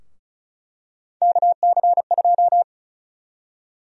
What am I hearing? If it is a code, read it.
Morse code, heard as a keyed tone, about 36 wpm, 700 Hz. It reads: KC2
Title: USA Call Sign Prefixes - 35wpm